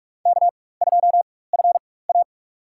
Morse code, heard as a keyed tone, about 44 wpm, spaced as for 21 wpm, 700 Hz. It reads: K2FA
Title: USA Call Signs - 4 Characters - 4x Character Spacing - 45wpm